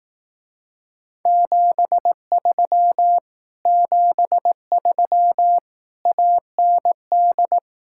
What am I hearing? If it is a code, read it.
Morse code, heard as a keyed tone, about 18 wpm, 700 Hz. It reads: 73 73 AND
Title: Speed-Racing Common QSO Phrases Encore - 15wpm